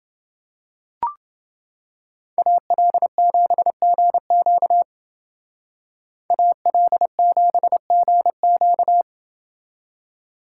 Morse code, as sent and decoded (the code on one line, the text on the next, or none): .- .-.. --... --. --.- / .- .-.. --... --. --.-
AL7GQ AL7GQ